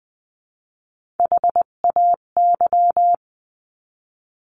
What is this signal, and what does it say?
Morse code, heard as a keyed tone, about 20 wpm, 700 Hz. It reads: HAY
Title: 258 Lesson 42c - Words 16-20wpm